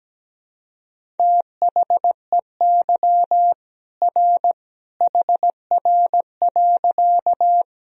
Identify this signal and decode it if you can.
Morse code, heard as a keyed tone, about 17 wpm, 700 Hz. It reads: THEY R HR.
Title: Sentences from Top 200 Words and Abbreviations 17wpm